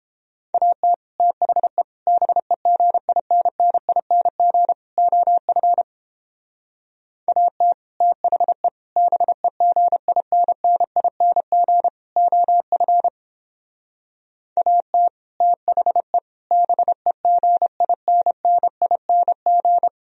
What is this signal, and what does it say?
Morse code, heard as a keyed tone, about 29 wpm, 700 Hz. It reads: AT THE BEGINNING OF AT THE BEGINNING OF AT THE BEGINNING